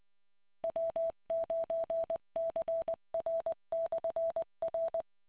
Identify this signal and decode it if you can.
Morse code, heard as a keyed tone, about 20 wpm, 660 Hz. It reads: W9CR/R